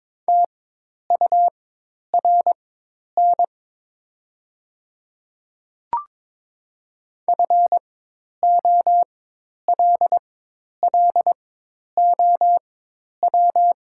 Morse code, heard as a keyed tone, about 22 wpm, spaced as for 10 wpm, 700 Hz. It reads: TURN FOLLOW
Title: 4x Character Spacing - Top 200 Words - 22wpm